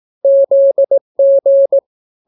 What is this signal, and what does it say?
Morse code, heard as a keyed tone, about 18 wpm, 550 Hz. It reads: ZG